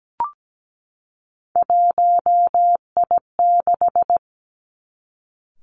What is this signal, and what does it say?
Morse code, heard as a keyed tone, about 17 wpm, 700 Hz. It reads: 1I6
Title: Three Challenging Letter-Number Combinations 17wpm